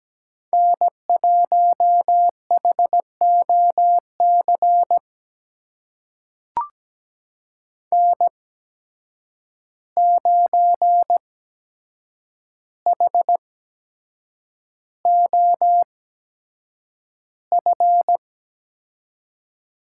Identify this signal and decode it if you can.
Morse code, heard as a keyed tone, about 17 wpm, spaced as for 7 wpm, 700 Hz. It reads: N1HOC N9HOF
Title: USA Call Signs - 5 Characters - 8x Character Spacing - 17wpm